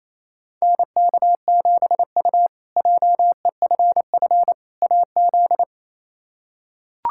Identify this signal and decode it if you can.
Morse code, heard as a keyed tone, about 28 wpm, 700 Hz. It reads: NK7U JEFF AZ
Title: K1USN SST QSO Practice - 28wpm